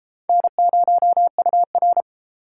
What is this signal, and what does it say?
Morse code, heard as a keyed tone, about 31 wpm, 700 Hz. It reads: N0UR